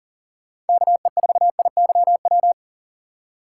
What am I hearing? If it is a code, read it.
Morse code, heard as a keyed tone, about 40 wpm, 700 Hz. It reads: KE4IYW
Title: USA Call Signs - 6 Characters - 6x Character Spacing - 40wpm